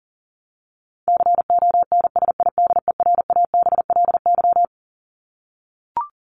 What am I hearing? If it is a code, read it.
Morse code, heard as a keyed tone, about 40 wpm, 700 Hz. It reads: CONSIDERABLY
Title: LY Suffix - Top 100 Words with Base Words 40wpm